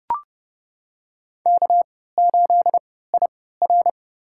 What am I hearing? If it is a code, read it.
Morse code, heard as a keyed tone, about 28 wpm, spaced as for 17 wpm, 700 Hz. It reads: K8IR
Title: USA Call Signs - 4 Characters - 3x Character Spacing - 30wpm